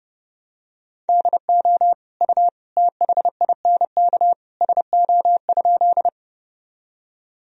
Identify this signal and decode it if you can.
Morse code, heard as a keyed tone, about 30 wpm, 700 Hz. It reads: DO U THINK SO?
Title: Sentences from Top 200 Words and Abbreviations 30wpm